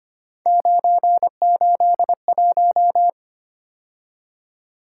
Morse code, heard as a keyed tone, about 25 wpm, 700 Hz. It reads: 981
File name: Three Letter-Number Combinations - 25wpm